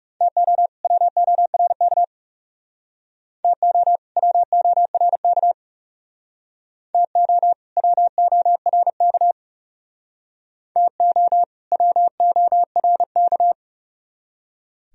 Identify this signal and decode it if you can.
Morse code, heard as a keyed tone, about 40 wpm, 700 Hz. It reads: TO WORK TO WORK TO WORK TO WORK